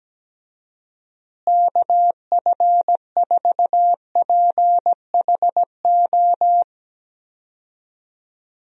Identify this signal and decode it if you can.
Morse code, heard as a keyed tone, about 17 wpm, 700 Hz. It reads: KF4PHO
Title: USA Call Signs - 6 Characters - Repeated 3x - 17wpm